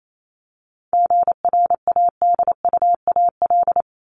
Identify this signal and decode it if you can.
Morse code, heard as a keyed tone, about 28 wpm, 700 Hz. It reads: GRADUAL
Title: LY Suffix - Top 100 Words with Base Words 28wpm